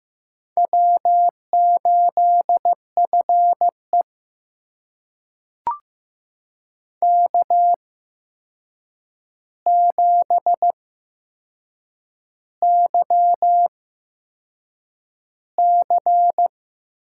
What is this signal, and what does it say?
Morse code, heard as a keyed tone, about 15 wpm, spaced as for 6 wpm, 700 Hz. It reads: W8FE K7YC
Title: USA Call Signs - 4 Characters - 8x Character Spacing - 15wpm